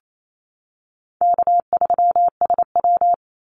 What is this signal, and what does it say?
Morse code, heard as a keyed tone, about 28 wpm, 700 Hz. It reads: K3SW
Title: Easy Call Signs - 28wpm